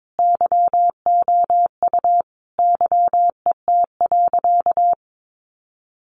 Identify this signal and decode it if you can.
Morse code, heard as a keyed tone, about 22 wpm, 700 Hz. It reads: YOU YET.
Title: A Sound Of Thunder 22wpm